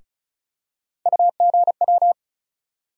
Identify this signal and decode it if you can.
Morse code, heard as a keyed tone, about 36 wpm, 700 Hz. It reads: UGW
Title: USA Call Sign Postfixes - 3 Letters 35wpm